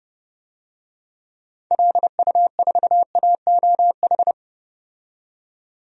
Morse code, heard as a keyed tone, about 30 wpm, 700 Hz. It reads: LU4AOH
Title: DXCC-100 Call Signs - 6 Characters - 30wpm